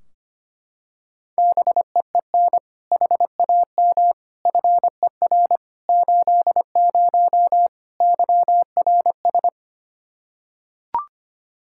Morse code, heard as a keyed tone, about 25 wpm, 700 Hz. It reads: BEEN HAM FER 80 YRS
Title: K1USN SST QSO Practice 25wpm